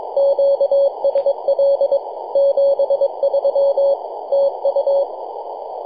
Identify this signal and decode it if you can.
Morse code, heard as a keyed tone, about 22 wpm, 560 Hz. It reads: QSL 73 TU
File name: yj0nk-18-075mhz-cw